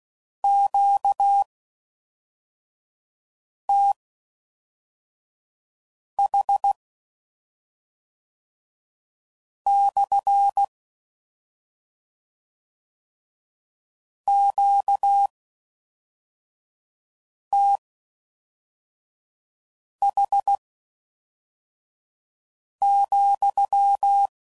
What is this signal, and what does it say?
Morse code, heard as a keyed tone, about 16 wpm, spaced as for 3 wpm, 800 Hz. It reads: QTH/ QTH,